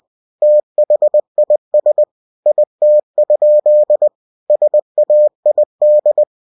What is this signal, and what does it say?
Morse code, heard as a keyed tone, about 20 wpm, 600 Hz. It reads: THIS IT? SAID
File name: cw-books / winnie-the-pooh / winnie-20wpm-page-0056